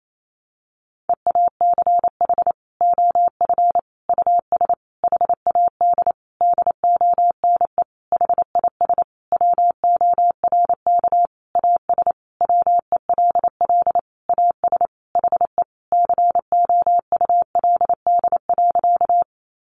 Morse code, heard as a keyed tone, about 28 wpm, 700 Hz. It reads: EACH OF US HAD DONE HIS WORK AS WELL AS HE COULD.
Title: Sentences from Top 500 Words 28wpm